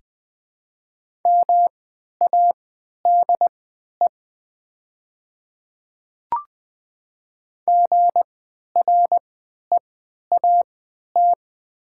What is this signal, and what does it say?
Morse code, heard as a keyed tone, about 20 wpm, spaced as for 11 wpm, 700 Hz. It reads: MADE GREAT
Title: 3x Character Spacing - Top 200 Words - 20wpm